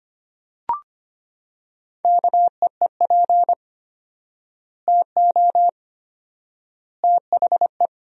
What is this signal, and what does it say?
Morse code, heard as a keyed tone, about 25 wpm, 700 Hz. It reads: KEEP TO THE
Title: 4x Word-Spacing - Sentences from Top 300 Words - 25wpm